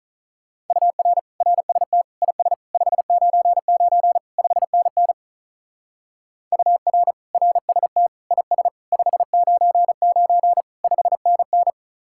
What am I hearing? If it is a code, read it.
Morse code, heard as a keyed tone, about 42 wpm, 700 Hz. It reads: UR RST IS 599 5NN UR RST IS 599 5NN
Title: Speed-Racing Common QSO Phrases - 35wpm